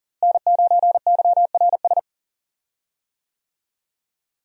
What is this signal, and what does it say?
Morse code, heard as a keyed tone, about 41 wpm, 700 Hz. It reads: N9YRS
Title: USA Call Signs - 5 Characters - Repeated 3x - 40wpm